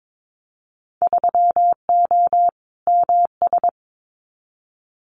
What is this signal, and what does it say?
Morse code, heard as a keyed tone, about 22 wpm, 700 Hz. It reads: 3O MS
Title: Winter Field Day Exchanges Encore 22wpm